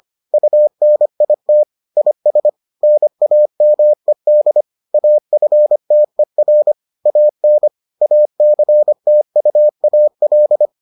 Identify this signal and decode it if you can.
Morse code, heard as a keyed tone, about 25 wpm, 600 Hz. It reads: UNIT IS NAMED AFTER AN ACTUAL